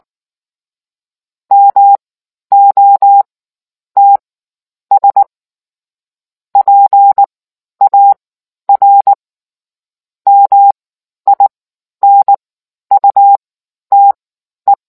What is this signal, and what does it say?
Morse code, heard as a keyed tone, about 19 wpm, spaced as for 10 wpm, 800 Hz. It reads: MOTS PAR MINUTE